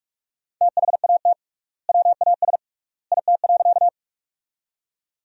Morse code, heard as a keyed tone, about 46 wpm, 690 Hz. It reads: THAT WAS IT.
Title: 3x Word-Spacing - Sentences from Top 100 Words - 45wpm